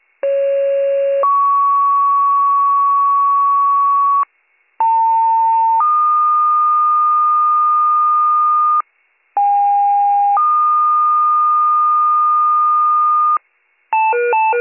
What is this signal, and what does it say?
Two-tone page, heard as a one-second tone, then a three-second tone, about 550 then 1120 Hz, about 900 then 1220 Hz, about 810 then 1220 Hz. It is heard from 0.2 to 13.4 s.